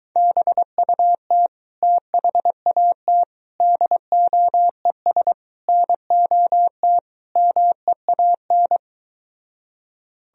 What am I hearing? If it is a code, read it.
Morse code, heard as a keyed tone, about 23 wpm, 700 Hz. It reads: BUT THAT DOES NOT MEAN